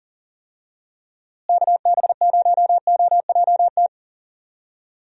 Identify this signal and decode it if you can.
Morse code, heard as a keyed tone, about 40 wpm, 700 Hz. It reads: KB0OJT
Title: USA Call Signs - 6 Characters - Repeated 2x - 40wpm